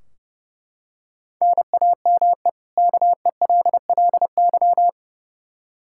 Morse code, heard as a keyed tone, about 32 wpm, 700 Hz. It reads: NAME KELLY
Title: K1USN SST QSO Practice 30wpm